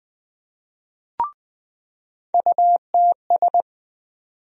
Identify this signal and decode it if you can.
Morse code, heard as a keyed tone, about 20 wpm, 700 Hz. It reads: UTS
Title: USA Call Sign Postfixes - 3 Letters 20wpm